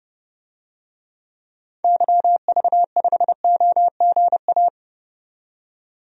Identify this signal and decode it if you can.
Morse code, heard as a keyed tone, about 30 wpm, 700 Hz. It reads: YV5OGA
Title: DXCC-100 Call Signs - 6 Characters - 30wpm